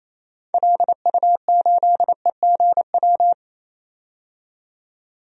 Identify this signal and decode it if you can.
Morse code, heard as a keyed tone, about 28 wpm, 700 Hz. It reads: LU8EGW
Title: DXCC-100 Call Signs - 6 Characters - 28wpm